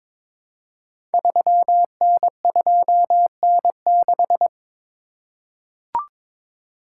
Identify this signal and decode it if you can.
Morse code, heard as a keyed tone, about 22 wpm, 700 Hz. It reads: 3N2N6
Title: Code Groups - 5 Letter:Numbers - 22wpm